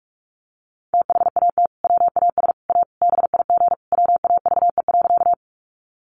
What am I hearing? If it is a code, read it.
Morse code, heard as a keyed tone, about 44 wpm, 700 Hz. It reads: THAT WAS A BIG WAVE.
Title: Sentences from Top 1400 Words 45wpm